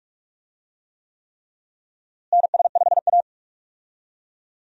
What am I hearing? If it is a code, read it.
Morse code, heard as a keyed tone, about 45 wpm, 700 Hz. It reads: NS5A
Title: USA Call Signs - 4 Characters - Repeated 3x - 45wpm